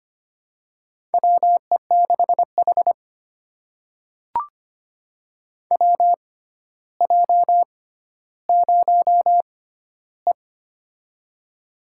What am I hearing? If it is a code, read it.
Morse code, heard as a keyed tone, about 25 wpm, spaced as for 13 wpm, 700 Hz. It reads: WE6H WJ0E
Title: USA Call Signs - 4 Characters - 6x Character Spacing - 25wpm